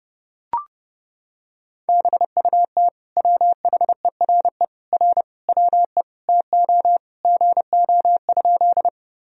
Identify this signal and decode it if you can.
Morse code, heard as a keyed tone, about 30 wpm, 700 Hz. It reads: BUT WHERE R WE TO GO?
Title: Sentences from Top 200 Words and Abbreviations 30wpm